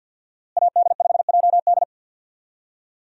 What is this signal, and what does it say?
Morse code, heard as a keyed tone, about 46 wpm, 700 Hz. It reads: AD5JD
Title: USA Call Signs - 5 Characters - Repeated 2x - 50wpm